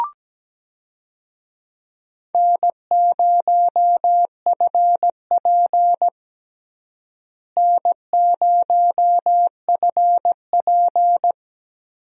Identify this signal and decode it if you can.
Morse code, heard as a keyed tone, about 17 wpm, 700 Hz. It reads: N0FP N0FP
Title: USA Call Signs - 4 Characters - Repeated 3x - 17wpm